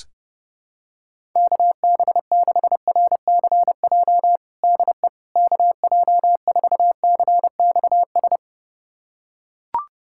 Morse code, heard as a keyed tone, about 30 wpm, 700 Hz. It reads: KB6RCJ DE KJ4CXS
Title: USA Callsign DE Callsign - 6 Characters - 30wpm